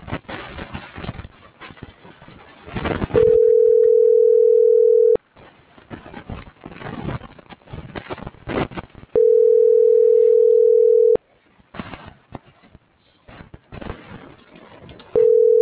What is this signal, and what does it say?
Ringback tone (North America), 440 + 480 Hz.